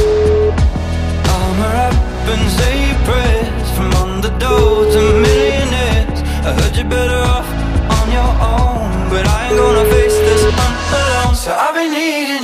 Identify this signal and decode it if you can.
Ringback tone (Europe, 425 Hz), with noise.